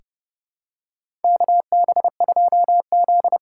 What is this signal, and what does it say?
Morse code, heard as a keyed tone, about 30 wpm, 700 Hz. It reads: KB2Z
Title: USA Call Signs - 6 Characters - 6x Character Spacing - 30wpm